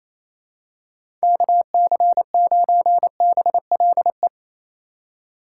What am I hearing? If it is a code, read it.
Morse code, heard as a keyed tone, about 28 wpm, 700 Hz. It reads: KC9BLE
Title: USA Call Signs - 6 Characters - 2x Character Spacing - 28wpm